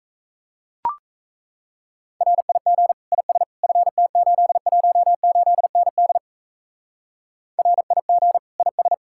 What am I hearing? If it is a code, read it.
Morse code, heard as a keyed tone, about 35 wpm, 700 Hz. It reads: RIG IS FT818ND RIG IS
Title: Speed-Racing Common QSO Phrases - 28wpm